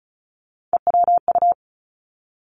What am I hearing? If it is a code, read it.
Morse code, heard as a keyed tone, about 36 wpm, 700 Hz. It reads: EWU
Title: Three Challenging Letter-Number Combinations 35wpm